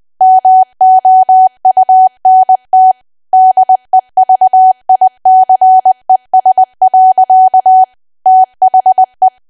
Morse code, heard as a keyed tone, about 20 wpm, 750 Hz. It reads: MOUNT DEVICES. THE
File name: cwa / W1AW-Level3 / 20WPM